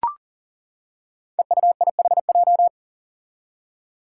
Morse code, heard as a keyed tone, about 37 wpm, 700 Hz. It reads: EUIHJ
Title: Five Challenging Letter-Number Combinations 40wpm